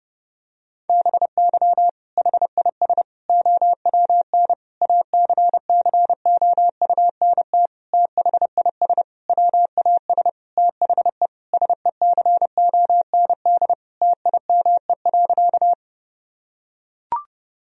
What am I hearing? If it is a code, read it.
Morse code, heard as a keyed tone, about 30 wpm, 700 Hz. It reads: BY HIS OWN ACCOUNT THIS WAS THE SECOND TIME.
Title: Sentences from Top 1000 Words - Review Part 1 - 30wpm